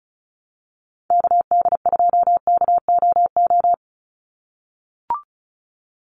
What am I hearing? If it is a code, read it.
Morse code, heard as a keyed tone, about 34 wpm, 700 Hz. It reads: KD2KOO
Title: USA Call Signs 35wpm